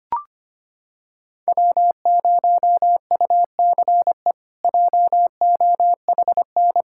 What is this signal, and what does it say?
Morse code, heard as a keyed tone, about 25 wpm, 700 Hz. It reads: W0UCE JOHN
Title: K1USN SST QSO Practice - 25wpm